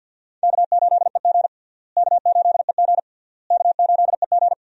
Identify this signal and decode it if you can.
Morse code, heard as a keyed tone, about 53 wpm, 700 Hz. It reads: K8EG K8EG K8EG
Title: USA Call Signs - 4 Characters - Repeated 3x - 50wpm